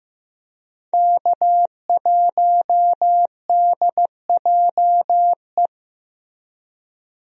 Morse code, heard as a keyed tone, about 15 wpm, 700 Hz. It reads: K1DJE
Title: USA Call Signs - 5 Characters - Repeated 3x - 15wpm